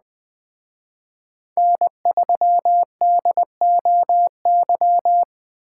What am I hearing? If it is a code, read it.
Morse code, heard as a keyed tone, about 21 wpm, 700 Hz. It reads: N3DOY
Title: Speed Racing - USA Call Signs 17wpm